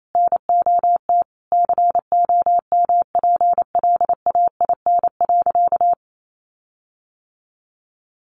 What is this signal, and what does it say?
Morse code, heard as a keyed tone, about 28 wpm, 700 Hz. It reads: NOT COMPLAIN.
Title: Sentences from Top 1300 Words 28wpm